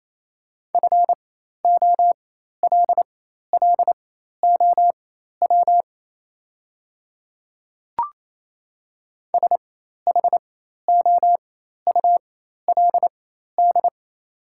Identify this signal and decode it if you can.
Morse code, heard as a keyed tone, about 28 wpm, spaced as for 13 wpm, 700 Hz. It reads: FOLLOW SHOULD
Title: 4x Character Spacing - Top 300 Words - 28wpm